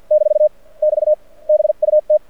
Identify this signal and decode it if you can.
Morse code, heard as a keyed tone, about 48 wpm, spaced as for 41 wpm, 610 Hz. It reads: - <BT> BUT
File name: archived / CRSnet